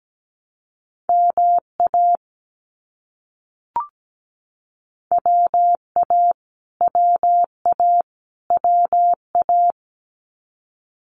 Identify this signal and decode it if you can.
Morse code, heard as a keyed tone, about 17 wpm, 700 Hz. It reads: MA WA WA WA
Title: US States Repeated 3x 17wpm